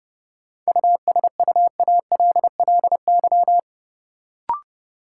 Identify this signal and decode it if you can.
Morse code, heard as a keyed tone, about 29 wpm, 700 Hz. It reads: USUALLY
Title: Sentences from Top 1000 Words - Review Part 3 - New Word Drill - 30wpm